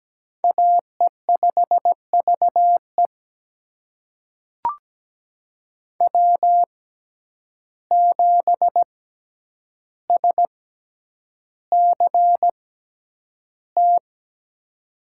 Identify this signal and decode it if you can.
Morse code, heard as a keyed tone, about 17 wpm, spaced as for 9 wpm, 700 Hz. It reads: AE5VE W7SCT
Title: USA Call Signs - 5 Characters - 6x Character Spacing - 17wpm